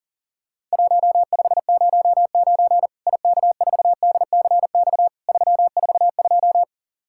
Morse code, heard as a keyed tone, about 43 wpm, 700 Hz. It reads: 1509 IK4DCX 342
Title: Worked All Europe DX Contest - QTC Exchange - 40wpm